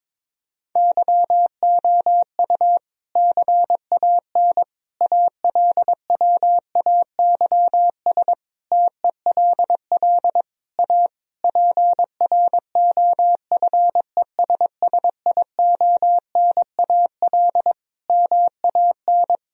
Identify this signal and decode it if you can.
Morse code, heard as a keyed tone, about 22 wpm, 700 Hz. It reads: YOU CAN ALWAYS TELL A PROFESSIONAL MAN
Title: Sentences from Top 1000 Words - Review Part 4 - 22wpm